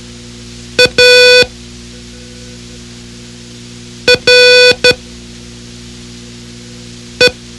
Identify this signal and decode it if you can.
Morse code, heard as a keyed tone, about 12 wpm, spaced as for 3 wpm, 500 Hz. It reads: ARE
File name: ham radio / CodeQuick / Code Quick 04